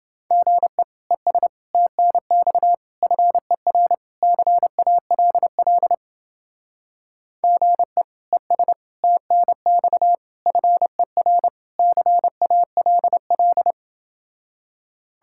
Morse code, heard as a keyed tone, about 29 wpm, 700 Hz. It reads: GE ES TNX FER CALL GE ES TNX FER CALL